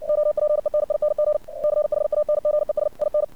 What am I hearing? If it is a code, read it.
Morse code, heard as a keyed tone, about 60 wpm, 600 Hz. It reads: OPENING CHANGES IN
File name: archived / CRSnet